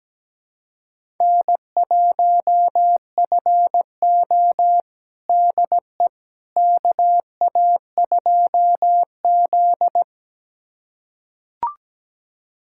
Morse code, heard as a keyed tone, about 17 wpm, 700 Hz. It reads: N1FO DE KA2Z
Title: USA Callsign DE Callsign - 4 Characters - 17wpm